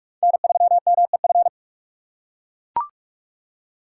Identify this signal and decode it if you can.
Morse code, heard as a keyed tone, about 42 wpm, 700 Hz. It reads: N3MEF
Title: USA Call Signs - 5 Characters - 4x Character Spacing - 45wpm